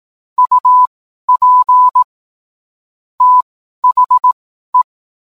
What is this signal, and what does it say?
Morse code, heard as a keyed tone, about 18 wpm, spaced as for 12 wpm, 1000 Hz. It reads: UP THE
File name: cwblog / cw mp3 directory / mem morse 12